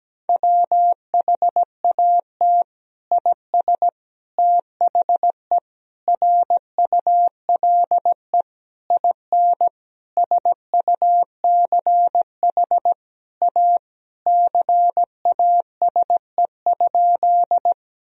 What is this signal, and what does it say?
Morse code, heard as a keyed tone, about 17 wpm, 700 Hz. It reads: WHAT IS THE RULE IN SUCH A CASE?
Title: Sentences from Top 1000 Words - Review Part 2 - 17wpm